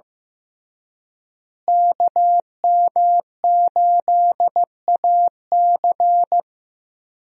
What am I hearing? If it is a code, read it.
Morse code, heard as a keyed tone, about 16 wpm, 700 Hz. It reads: KM8AC
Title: USA Call Signs - 5 Characters - Repeated 3x - 15wpm